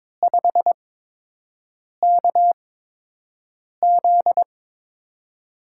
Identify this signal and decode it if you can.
Morse code, heard as a keyed tone, about 22 wpm, spaced as for 6 wpm, 700 Hz. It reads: 5KZ
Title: USA Call Signs - 5 Characters - 8x Character Spacing - 22wpm